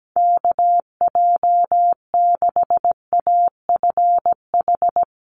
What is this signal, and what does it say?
Morse code, heard as a keyed tone, about 17 wpm, 700 Hz. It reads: KJ6AFH
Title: USA Call Signs Repeated 3x 17wpm